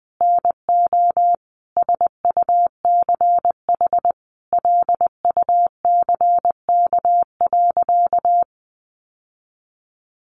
Morse code, heard as a keyed tone, about 20 wpm, 700 Hz. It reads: NO SUCH LUCK.
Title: Sentences from Top 1500 Words 20wpm